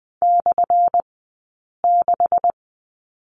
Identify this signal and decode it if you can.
Morse code, heard as a keyed tone, about 20 wpm, spaced as for 8 wpm, 700 Hz. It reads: /6
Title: 177 Lesson 29d - Slash - Two Characters 20wpm